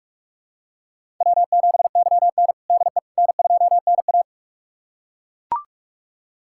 Morse code, heard as a keyed tone, about 41 wpm, 700 Hz. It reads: W7YN DE N2NA
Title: USA Callsign DE Callsign - 4 Characters - 45wpm